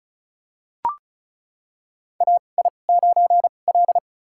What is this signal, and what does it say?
Morse code, heard as a keyed tone, about 36 wpm, spaced as for 25 wpm, 700 Hz. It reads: AI9L
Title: USA Call Signs - 2x Character Spacing - 35wpm